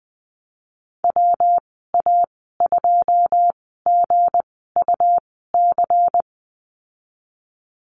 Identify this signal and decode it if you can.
Morse code, heard as a keyed tone, about 20 wpm, spaced as for 15 wpm, 700 Hz. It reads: WA2GUC
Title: 193 Lesson 31g - Letter v - Call Signs 20wpm